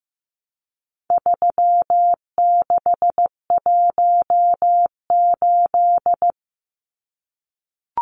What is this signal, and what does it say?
Morse code, heard as a keyed tone, about 15 wpm, 700 Hz. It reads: 3618
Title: Numbers - 4-Digit 15wpm